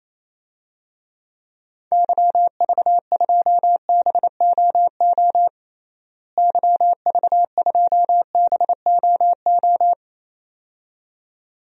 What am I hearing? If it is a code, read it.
Morse code, heard as a keyed tone, about 27 wpm, 700 Hz. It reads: YV2BOO YV2BOO
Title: DXCC-100 Call Signs - 6 Characters - Repeated 2x - 28wpm